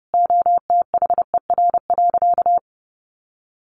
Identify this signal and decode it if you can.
Morse code, heard as a keyed tone, about 30 wpm, 700 Hz. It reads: OTHER.
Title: Sentences from Top 1300 Words 30wpm